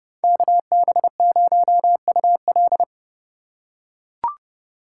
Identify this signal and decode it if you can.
Morse code, heard as a keyed tone, about 31 wpm, 700 Hz. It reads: KB0UL